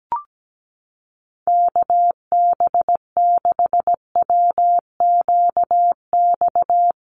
Morse code, heard as a keyed tone, about 17 wpm, 700 Hz. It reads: KB6WQX
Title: USA Call Signs Repeated 3x 17wpm